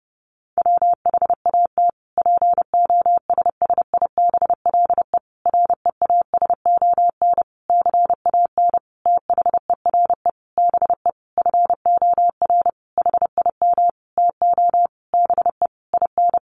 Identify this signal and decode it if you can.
Morse code, heard as a keyed tone, about 30 wpm, 700 Hz. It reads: WHAT POSSIBLE REASON CAN THERE BE FOR HIM TO BE IN
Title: Sentences from Top 1300 Words 30wpm